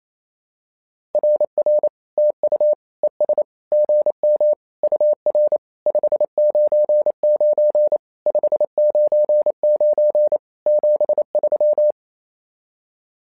Morse code, heard as a keyed tone, about 27 wpm, 600 Hz. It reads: RR TU ES GM UR 599 599 73
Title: SOTA QSO Practice 28wpm